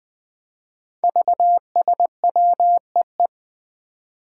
Morse code, heard as a keyed tone, about 20 wpm, 700 Hz. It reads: VSWEE
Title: Five Challenging Letter-Number Combinations 20wpm